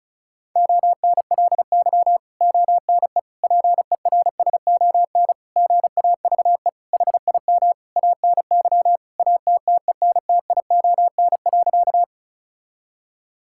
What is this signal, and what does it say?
Morse code, heard as a keyed tone, about 36 wpm, 700 Hz. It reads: ONLY ONE PERSON GAVE HIM ANY ATTENTION.
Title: Sentences from Top 1000 Words - Review Part 6 - 35wpm